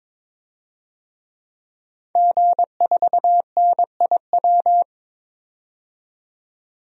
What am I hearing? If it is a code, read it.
Morse code, heard as a keyed tone, about 21 wpm, 700 Hz. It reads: G4NIW
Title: DXCC-100 Call Signs - 5 Characters - 22wpm